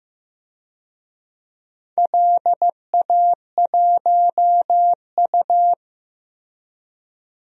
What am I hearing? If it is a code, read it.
Morse code, heard as a keyed tone, about 15 wpm, 700 Hz. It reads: LA1U
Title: DXCC-100 Call Signs - 4 Characters - Repeated 3x - 15wpm